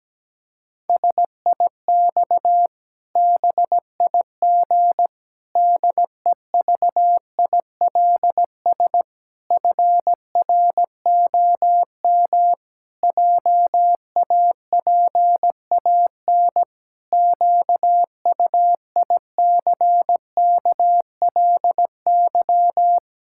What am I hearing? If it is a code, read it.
Morse code, heard as a keyed tone, about 17 wpm, 700 Hz. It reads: SIX BIG DEVILS FROM JAPAN QUICKLY
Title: Pangram Challenge v2 - 17wpm